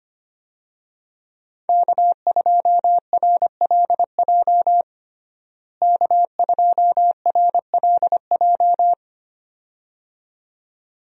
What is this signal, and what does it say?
Morse code, heard as a keyed tone, about 25 wpm, 700 Hz. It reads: K2RLJ K2RLJ
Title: USA Call Signs - 5 Characters - Repeated 2x - 25wpm